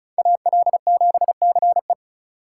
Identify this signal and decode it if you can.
Morse code, heard as a keyed tone, about 36 wpm, 700 Hz. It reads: AL7CE
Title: USA Call Signs - 5 Characters - 2x Character Spacing - 35wpm